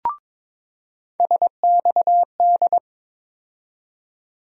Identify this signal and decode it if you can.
Morse code, heard as a keyed tone, about 22 wpm, 700 Hz. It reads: SXD
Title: USA Call Sign Postfixes - 3 Letters 22wpm